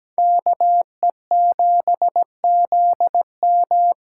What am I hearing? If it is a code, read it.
Morse code, heard as a keyed tone, about 17 wpm, 700 Hz. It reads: KE7ZM
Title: USA Call Signs - 8x Character Spacing - 17wpm